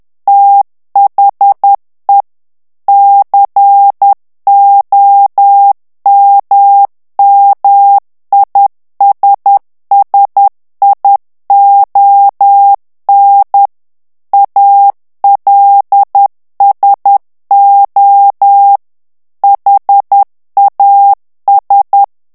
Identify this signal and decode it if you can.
Morse code, heard as a keyed tone, about 10 wpm, 800 Hz. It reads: THE COMMISSION ALSO HAS
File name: ANGLAIS texte clair 3 vitesse 10 mots minute